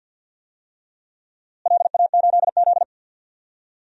Morse code, heard as a keyed tone, about 48 wpm, 700 Hz. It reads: LA8Z
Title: DXCC-100 Call Signs - 4 Characters - Repeated 2x - 50wpm